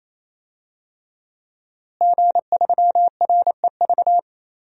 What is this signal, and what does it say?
Morse code, heard as a keyed tone, about 28 wpm, 700 Hz. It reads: G3REV
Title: DXCC-100 Call Signs - 5 Characters - Repeated 3x - 28wpm